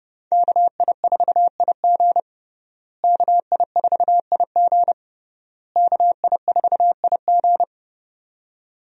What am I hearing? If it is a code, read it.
Morse code, heard as a keyed tone, about 30 wpm, 700 Hz. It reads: KI4IG KI4IG KI4IG
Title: USA Call Signs - 5 Characters - Repeated 3x - 30wpm